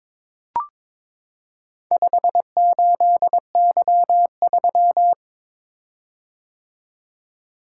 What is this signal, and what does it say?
Morse code, heard as a keyed tone, about 22 wpm, 690 Hz. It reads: 58Y3
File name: practice / 22wpm / Four Letter-Number Combinations - 22wpm